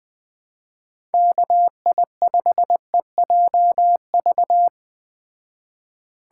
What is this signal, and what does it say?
Morse code, heard as a keyed tone, about 20 wpm, 700 Hz. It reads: KI5EJV
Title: Speed Racing - USA Call Signs 17wpm